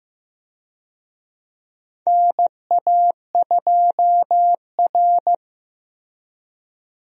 Morse code, heard as a keyed tone, about 15 wpm, 700 Hz. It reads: NA2R
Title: USA Call Signs - 4 Characters - Repeated 3x - 15wpm